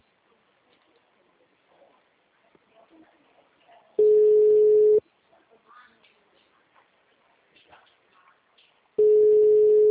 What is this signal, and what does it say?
Ringback tone (Europe, 425 Hz).